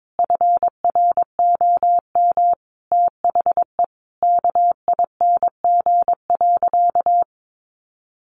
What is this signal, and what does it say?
Morse code, heard as a keyed tone, about 22 wpm, 700 Hz. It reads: FROM THE KING.
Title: Sentences from Top 1400 Words 22wpm